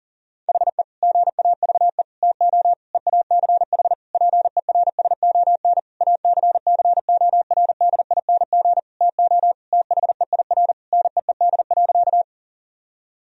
Morse code, heard as a keyed tone, about 40 wpm, 700 Hz. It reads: HE GAVE TO EACH PERSON ACCORDING TO THEIR NEED.
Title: Sentences from Top 1000 Words - Review Part 4 - 40wpm